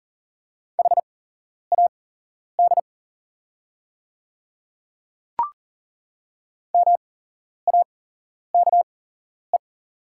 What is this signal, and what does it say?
Morse code, heard as a keyed tone, about 40 wpm, spaced as for 11 wpm, 690 Hz. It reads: HAD MAKE
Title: 8x Character Spacing - Top 100 Words - 40wpm